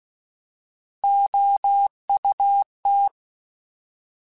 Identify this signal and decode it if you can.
Morse code, heard as a keyed tone, about 16 wpm, 800 Hz. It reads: OUT
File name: Alphabetical / Alphabetical mp3s / Os @15